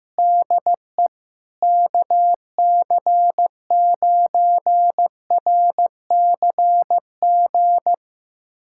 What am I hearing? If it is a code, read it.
Morse code, heard as a keyed tone, about 15 wpm, 700 Hz. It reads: DE KC9RCG
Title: USA Callsign DE Callsign - 6 Characters - 15wpm